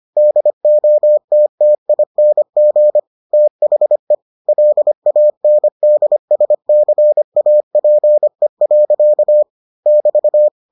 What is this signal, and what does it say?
Morse code, heard as a keyed tone, about 25 wpm, 600 Hz. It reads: DOTTING THE LANDSCAPE. <BT>